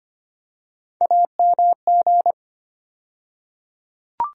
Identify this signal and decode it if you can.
Morse code, heard as a keyed tone, about 25 wpm, 700 Hz. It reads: AMG
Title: USA Call Sign Postfixes - 3 Letters 25wpm